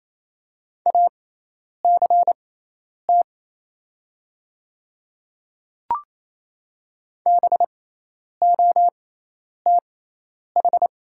Morse code, heard as a keyed tone, about 28 wpm, spaced as for 10 wpm, 700 Hz. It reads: ACT BOTH
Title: 6x Character Spacing - Top 300 Words - 28wpm